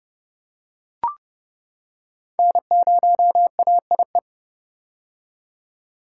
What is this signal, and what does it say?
Morse code, heard as a keyed tone, about 30 wpm, 700 Hz. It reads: N0AIE
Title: K1USN SST QSO Practice - 30wpm